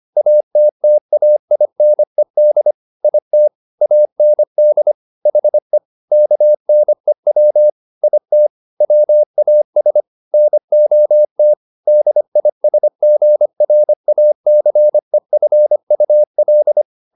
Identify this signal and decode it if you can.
Morse code, heard as a keyed tone, about 25 wpm, 600 Hz. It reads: ATTAINED IT AND HE KNEW IT WAS NOT DISGRACEFUL
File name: the old man and the sea part 1 25wpm